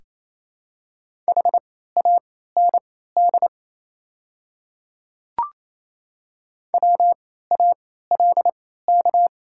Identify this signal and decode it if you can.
Morse code, heard as a keyed tone, about 27 wpm, spaced as for 16 wpm, 700 Hz. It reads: HAND WALK
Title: 3x Character Spacing - Top 300 Words - 28wpm